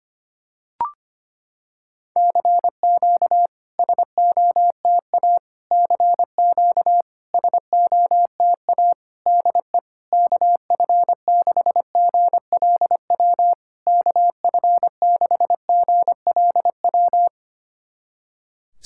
Morse code, heard as a keyed tone, about 25 wpm, 700 Hz. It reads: CQ SOTA CQ SOTA DE KF6GLW KF6GLW
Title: SOTA QSO Practice 25wpm